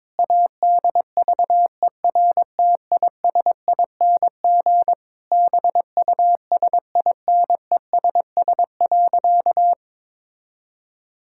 Morse code, heard as a keyed tone, about 22 wpm, 700 Hz. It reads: ADVERTISING BUSINESS.
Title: Sentences from Top 1000 Words - Review Part 6 - 22wpm